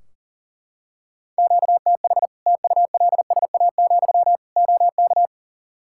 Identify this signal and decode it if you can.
Morse code, heard as a keyed tone, about 40 wpm, 700 Hz. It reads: QTH TULSA, OK
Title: Practice QSO - Exchange 1 - Beginning - 40wpm